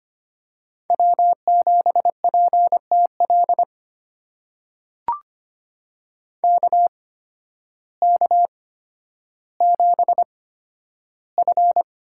Morse code, heard as a keyed tone, about 25 wpm, spaced as for 12 wpm, 700 Hz. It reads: W7PTL KK7F
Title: USA Call Signs - 5 Characters - 8x Character Spacing - 25wpm